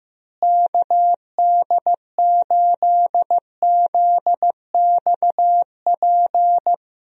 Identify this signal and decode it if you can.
Morse code, heard as a keyed tone, about 15 wpm, 700 Hz. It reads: KD8ZXP
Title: USA Call Signs - 6 Characters - Repeated 3x - 15wpm